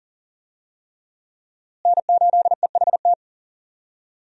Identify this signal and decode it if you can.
Morse code, heard as a keyed tone, about 40 wpm, 700 Hz. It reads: N8EHT
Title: USA Call Signs - 5 Characters - Repeated 3x - 40wpm